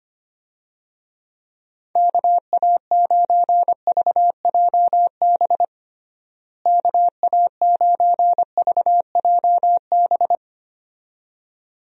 Morse code, heard as a keyed tone, about 25 wpm, 700 Hz. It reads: KA9VJB KA9VJB
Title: USA Call Signs - 6 Characters - Repeated 2x - 25wpm